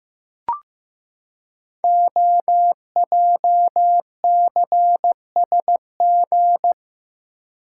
Morse code, heard as a keyed tone, about 15 wpm, 700 Hz. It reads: OJCSG
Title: Code Groups - 5 Letters - 15wpm